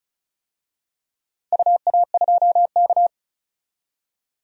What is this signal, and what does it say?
Morse code, heard as a keyed tone, about 36 wpm, 700 Hz. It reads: UA2K
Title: DXCC-100 Call Signs - 4 Characters - 35wpm